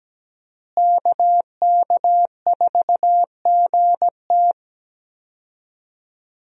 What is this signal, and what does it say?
Morse code, heard as a keyed tone, about 17 wpm, 700 Hz. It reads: KK4GT
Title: USA Call Signs - 5 Characters - Repeated 2x - 17wpm